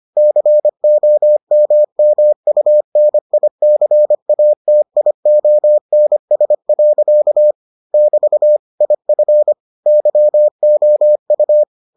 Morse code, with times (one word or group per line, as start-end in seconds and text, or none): 0.2-7.5 COMMUNICATIONS.
7.9-8.6 <BT>
8.8-9.5 IF
9.9-11.6 YOU